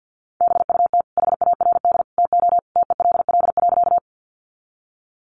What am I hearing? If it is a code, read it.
Morse code, heard as a keyed tone, about 50 wpm, 700 Hz. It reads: BUT HARD TO TELL.
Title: Sentences from Top 300 Words 50wpm